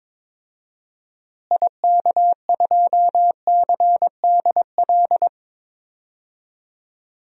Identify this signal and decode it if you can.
Morse code, heard as a keyed tone, about 22 wpm, 700 Hz. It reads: IK2CDL
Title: DXCC-100 Call Signs - 6 Characters - Repeated 3x - 22wpm